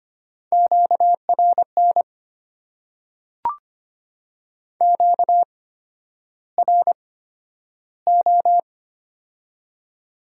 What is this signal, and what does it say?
Morse code, heard as a keyed tone, about 25 wpm, spaced as for 11 wpm, 700 Hz. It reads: QRN QRO
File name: Top 90 QSO Elements - 8x Character Spacing - 25wpm